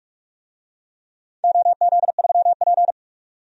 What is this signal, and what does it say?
Morse code, heard as a keyed tone, about 46 wpm, 700 Hz. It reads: OZ3P
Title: DXCC-100 Call Signs - 4 Characters - 45wpm